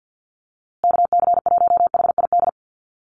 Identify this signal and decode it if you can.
Morse code, heard as a keyed tone, about 54 wpm, 700 Hz. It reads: KC1HID
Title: USA Call Signs Repeated 3x 50wpm